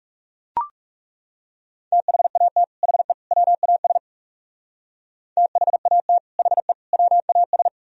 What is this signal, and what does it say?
Morse code, heard as a keyed tone, about 36 wpm, 700 Hz. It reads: THAT HE WAS THAT HE WAS
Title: Speed-Racing - Sets of 3 Words 30wpm